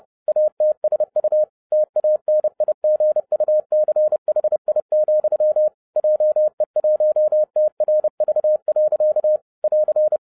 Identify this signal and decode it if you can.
Morse code, heard as a keyed tone, about 30 wpm, 610 Hz. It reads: ATSU TANIGUCHI, JE1TRV. <AR>